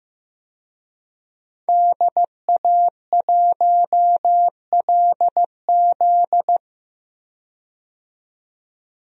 Morse code, heard as a keyed tone, about 15 wpm, 700 Hz. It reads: DA1LZ
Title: USA Call Signs - 5 Characters - 15wpm